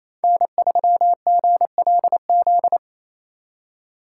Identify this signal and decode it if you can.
Morse code, heard as a keyed tone, about 33 wpm, 700 Hz. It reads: N3GLZ
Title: Speed Racing - USA Call Signs 28wpm